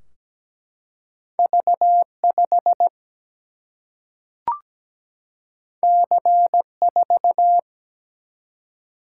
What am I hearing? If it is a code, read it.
Morse code, heard as a keyed tone, about 17 wpm, 700 Hz. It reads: V5 C4
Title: Single-Letter Single-Number 17wpm